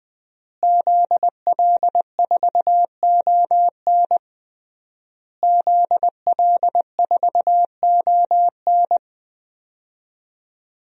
Morse code, heard as a keyed tone, about 20 wpm, 700 Hz. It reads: ZL4ON ZL4ON
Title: DXCC-100 Call Signs - 5 Characters - Repeated 3x - 20wpm